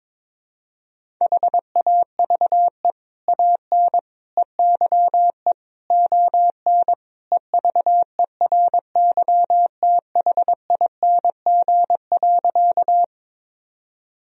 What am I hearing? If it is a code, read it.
Morse code, heard as a keyed tone, about 21 wpm, 700 Hz. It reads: HAVE AN EYE ON EVERYTHING.
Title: Sentences from Top 1000 Words - Review Part 1 - 22wpm